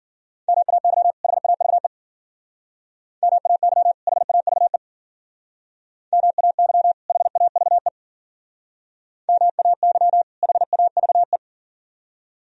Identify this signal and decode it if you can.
Morse code, heard as a keyed tone, about 51 wpm, 700 Hz. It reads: MAY HAVE MAY HAVE MAY HAVE MAY HAVE